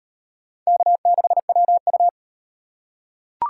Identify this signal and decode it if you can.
Morse code, heard as a keyed tone, about 36 wpm, 700 Hz. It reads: K6WU